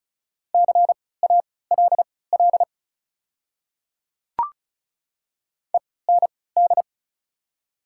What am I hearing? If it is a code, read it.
Morse code, heard as a keyed tone, about 34 wpm, spaced as for 20 wpm, 700 Hz. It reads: CALL END
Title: 3x Character Spacing - Top 200 Words - 35wpm